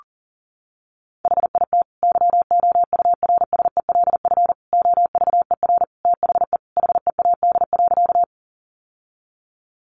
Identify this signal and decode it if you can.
Morse code, heard as a keyed tone, about 40 wpm, 700 Hz. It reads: HIT YOURSELF OVER THE HEAD.
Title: Sentences from Top 1200 Words 40wpm